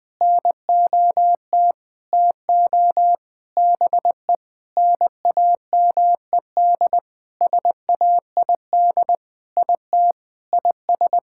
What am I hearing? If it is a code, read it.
Morse code, heard as a keyed tone, about 20 wpm, 700 Hz. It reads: NOT TO BE NAMED SAID IT IS
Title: Sentences with 7-Word Phrases - Spoken Beginning - 20wpm